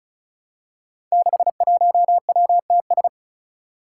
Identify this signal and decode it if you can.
Morse code, heard as a keyed tone, about 36 wpm, 700 Hz. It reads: 61WTS
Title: Five Challenging Letter-Number Combinations 35wpm